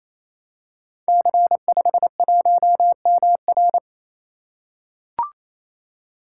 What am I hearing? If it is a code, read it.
Morse code, heard as a keyed tone, about 29 wpm, 700 Hz. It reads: C51MR
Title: Code Groups - 5 Letter:Numbers - 28wpm